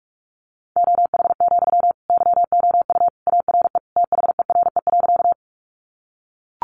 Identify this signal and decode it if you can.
Morse code, heard as a keyed tone, about 44 wpm, 700 Hz. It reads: OH, YOU ARE THERE.
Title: Sentences from Top 500 Words 45wpm